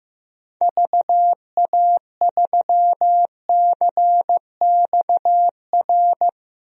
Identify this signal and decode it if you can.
Morse code, heard as a keyed tone, about 15 wpm, 700 Hz. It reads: VA3CXR
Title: DXCC-100 Call Signs - 6 Characters - 15wpm